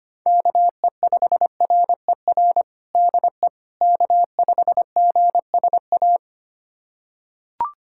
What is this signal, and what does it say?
Morse code, heard as a keyed tone, about 25 wpm, 700 Hz. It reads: KE5RER DE K5GSA
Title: USA Callsign DE Callsign 25wpm